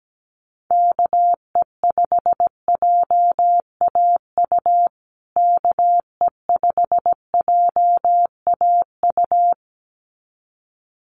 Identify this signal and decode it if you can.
Morse code, heard as a keyed tone, about 17 wpm, 700 Hz. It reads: KE5JAU KE5JAU
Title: USA Call Signs Repeated 2x 17wpm